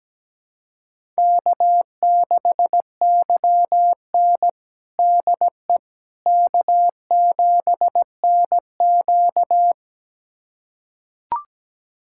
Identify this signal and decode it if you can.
Morse code, heard as a keyed tone, about 17 wpm, 700 Hz. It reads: K6YN DE K7NQ
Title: USA Callsign DE Callsign - 4 Characters - 17wpm